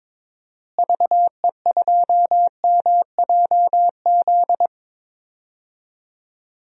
Morse code, heard as a keyed tone, about 22 wpm, 700 Hz. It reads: VE2MJZ
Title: DXCC-100 Call Signs - 6 Characters - Repeated 2x - 22wpm